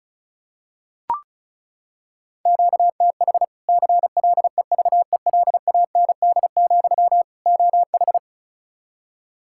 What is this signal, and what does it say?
Morse code, heard as a keyed tone, about 34 wpm, 700 Hz. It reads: QTH CLEVELAND, OH
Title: Practice QSO - Exchange 1 - Beginning - 35wpm